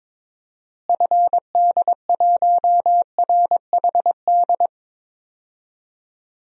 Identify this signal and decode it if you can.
Morse code, heard as a keyed tone, about 22 wpm, 700 Hz. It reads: FD1RHD
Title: DXCC-100 Call Signs - 6 Characters - Repeated 2x - 22wpm